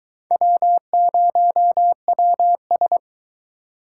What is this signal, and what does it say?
Morse code, heard as a keyed tone, about 22 wpm, 700 Hz. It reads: W0WS